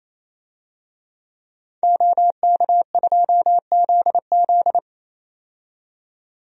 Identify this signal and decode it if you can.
Morse code, heard as a keyed tone, about 28 wpm, 700 Hz. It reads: OK2ZZ
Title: DXCC-100 Call Signs - 5 Characters - 28wpm